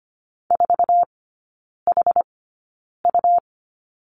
Morse code, heard as a keyed tone, about 25 wpm, spaced as for 9 wpm, 700 Hz. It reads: 4HU
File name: Lesson 17e - Letter u - Three Characters 25wpm